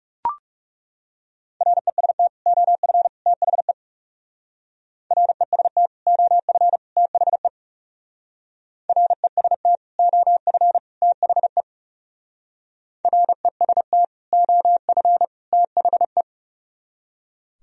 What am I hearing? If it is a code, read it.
Morse code, heard as a keyed tone, about 37 wpm, 700 Hz. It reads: REST OF THE REST OF THE REST OF THE REST OF THE
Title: Speed-Racing - Sets of 3 Words 30wpm